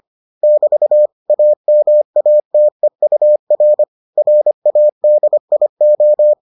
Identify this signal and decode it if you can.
Morse code, heard as a keyed tone, about 25 wpm, 600 Hz. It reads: <BT> AMATEUR RADIO